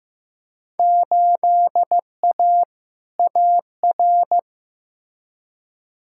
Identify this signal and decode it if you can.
Morse code, heard as a keyed tone, about 15 wpm, 700 Hz. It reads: 8A AR
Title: Field Day Exchanges Encore - 15wpm